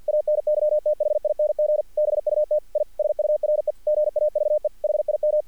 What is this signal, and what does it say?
Morse code, heard as a keyed tone, about 50 wpm, 600 Hz. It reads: ANYTHING BUT I SURE GAVE HIM
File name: archived / CRSnet